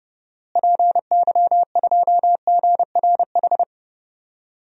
Morse code, heard as a keyed tone, about 30 wpm, 700 Hz. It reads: PY2GRH
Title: DXCC-100 Call Signs - 6 Characters - Repeated 2x - 30wpm